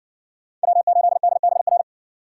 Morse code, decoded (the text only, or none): W8DBR